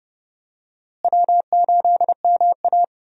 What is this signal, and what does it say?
Morse code, heard as a keyed tone, about 32 wpm, 700 Hz. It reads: W8MA
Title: USA Call Signs - 4 Characters - Repeated 2x - 30wpm